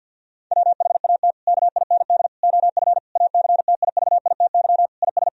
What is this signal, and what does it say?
Morse code, heard as a keyed tone, about 50 wpm, 700 Hz. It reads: WHAT KIND OF ACTIVITY IS
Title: Sentences from Top 1000 Words - Review Part 4 - 50wpm